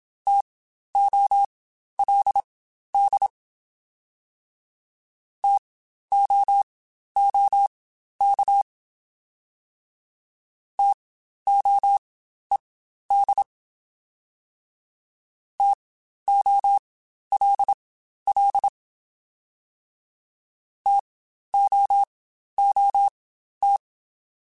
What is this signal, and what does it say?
Morse code, heard as a keyed tone, about 27 wpm, spaced as for 12 wpm, 800 Hz. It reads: TOLD TOOK TOED TOLL TOOT
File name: Spaced Alphas / Spaced Alphas mp3s / Ts sp @25